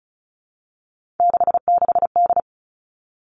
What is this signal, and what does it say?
Morse code, heard as a keyed tone, about 36 wpm, 700 Hz. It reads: 66D
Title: Three Challenging Letter-Number Combinations 35wpm